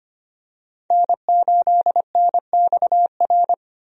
Morse code, heard as a keyed tone, about 25 wpm, 700 Hz. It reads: N8NXR
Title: USA Call Signs - 5 Characters - 25wpm